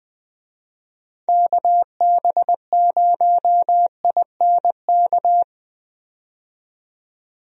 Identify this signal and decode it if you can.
Morse code, heard as a keyed tone, about 20 wpm, 700 Hz. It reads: KB0INK
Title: USA Call Signs - 6 Characters - 20wpm